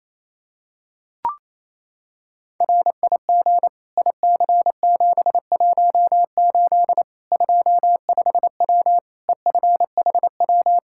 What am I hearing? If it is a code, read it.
Morse code, heard as a keyed tone, about 28 wpm, 700 Hz. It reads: RIG IC718 25W EFHW
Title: Practice QSO - Exchange 2 - Info - 28wpm